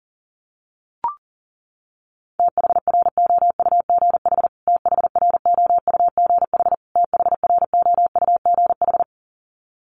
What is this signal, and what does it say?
Morse code, heard as a keyed tone, about 37 wpm, 700 Hz. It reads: THROUGH THROUGH THROUGH
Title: Top 300 Words Repeated 40wpm